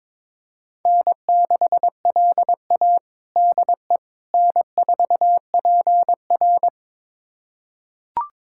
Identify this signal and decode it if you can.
Morse code, heard as a keyed tone, about 22 wpm, 700 Hz. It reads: N6LA DE N4PR
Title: USA Callsign DE Callsign - 4 Characters - 22wpm